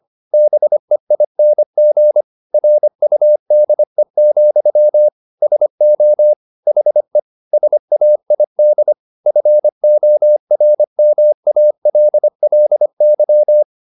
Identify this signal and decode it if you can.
Morse code, heard as a keyed tone, about 25 wpm, 600 Hz. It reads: BEING RUDE, SO HE SAID FORMALLY